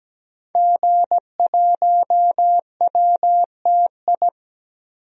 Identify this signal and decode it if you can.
Morse code, heard as a keyed tone, about 17 wpm, 700 Hz. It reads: G1WTI
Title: DXCC-100 Call Signs - 5 Characters - Repeated 2x - 17wpm